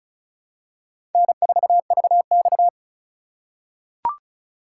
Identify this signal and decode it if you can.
Morse code, heard as a keyed tone, about 36 wpm, 690 Hz. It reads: N4VX
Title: USA Call Signs - 4 Characters - 2x Character Spacing - 35wpm